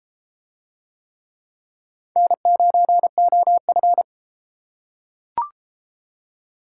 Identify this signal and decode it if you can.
Morse code, heard as a keyed tone, about 33 wpm, 700 Hz. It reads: N9OF